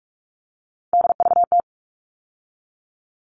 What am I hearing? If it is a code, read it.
Morse code, heard as a keyed tone, about 44 wpm, 700 Hz. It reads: DVT
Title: All Cut Numbers - 3-Digit 45wpm